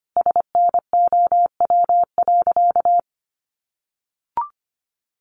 Morse code, heard as a keyed tone, about 25 wpm, 700 Hz. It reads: SNOW.
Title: Easy Phrases 25wpm